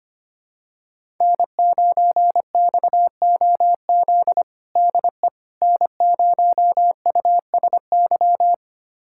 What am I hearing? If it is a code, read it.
Morse code, heard as a keyed tone, about 25 wpm, 700 Hz. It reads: N9XOZ DE N0USY
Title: USA Callsign DE Callsign - 5 Characters - 25wpm